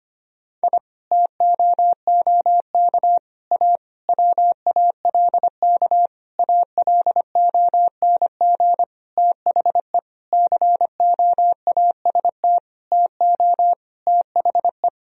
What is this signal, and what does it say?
Morse code, heard as a keyed tone, about 25 wpm, 700 Hz. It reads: I TOOK A WALK ALONG THE COAST TO THE
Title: Sentences from Top 1000 Words - Review Part 4 - 25wpm